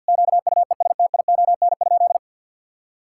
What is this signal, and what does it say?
Morse code, heard as a keyed tone, about 50 wpm, 700 Hz. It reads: QUESTION?
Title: Sentences from Top 1000 Words - Review Part 5 - 50wpm